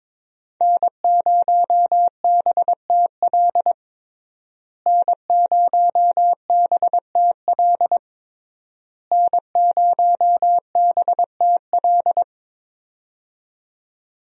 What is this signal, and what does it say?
Morse code, heard as a keyed tone, about 23 wpm, 700 Hz. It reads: N0BTL N0BTL N0BTL
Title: USA Call Signs - 5 Characters - Repeated 3x - 22wpm